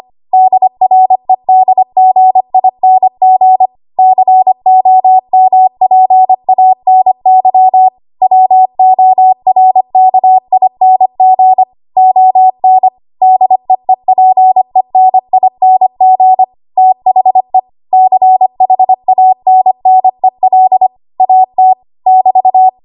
Morse code, heard as a keyed tone, about 25 wpm, 750 Hz. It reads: DREDGING COMPANY WORKING ON DEEPENING THE CHANNEL AT <BT>
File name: Archive / 25 WPM / 25WPM